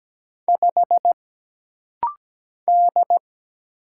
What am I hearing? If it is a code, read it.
Morse code, heard as a keyed tone, about 17 wpm, 700 Hz. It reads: 5 D
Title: Single Letter-Number - Rapid-Fire 17wpm